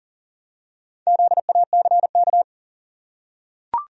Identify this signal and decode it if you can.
Morse code, heard as a keyed tone, about 37 wpm, 700 Hz. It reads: ZACK
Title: USA Common Names - 40wpm